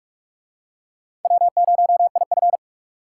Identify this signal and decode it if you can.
Morse code, heard as a keyed tone, about 44 wpm, 700 Hz. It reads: W0IF